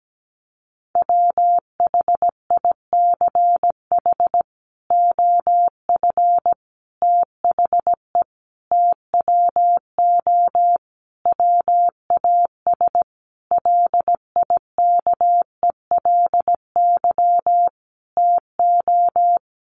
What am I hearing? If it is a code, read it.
Morse code, heard as a keyed tone, about 17 wpm, 700 Hz. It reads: WHICH OF THE TWO WAS LIKELY TO
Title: Sentences from Top 1300 Words 17wpm